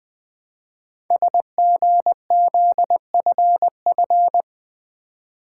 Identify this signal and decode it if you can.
Morse code, heard as a keyed tone, about 20 wpm, 700 Hz. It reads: SGZFF
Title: Code Groups - 5 Letters - 20wpm